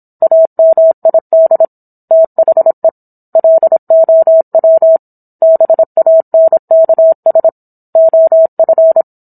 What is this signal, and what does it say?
Morse code, heard as a keyed tone, about 27 wpm, 650 Hz. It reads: AMID THE LOW BANKS OF
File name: Book One / 17-Chapter 17 - 26 WPM